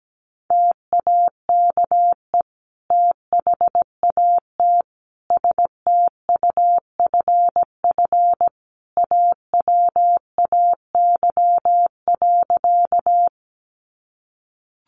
Morse code, heard as a keyed tone, about 17 wpm, 700 Hz. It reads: TAKE THAT STUFF AWAY.
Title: Sentences from Top 1400 Words 17wpm